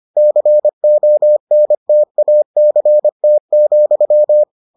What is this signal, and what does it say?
Morse code, heard as a keyed tone, about 25 wpm, 600 Hz. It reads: CONTACT,